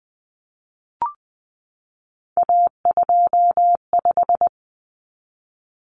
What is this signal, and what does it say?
Morse code, heard as a keyed tone, about 20 wpm, 700 Hz. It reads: A25
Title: Common Cut Numbers - 3-Digit 20wpm